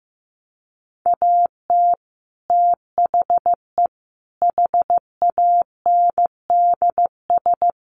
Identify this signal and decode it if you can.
Morse code, heard as a keyed tone, about 15 wpm, 700 Hz. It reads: AT THE HANDS
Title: Sets of 5 Words Encore2 15wpm